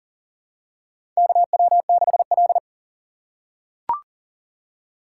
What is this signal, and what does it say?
Morse code, heard as a keyed tone, about 40 wpm, 700 Hz. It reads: KW6L
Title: USA Call Signs - 4 Characters - 6x Character Spacing - 40wpm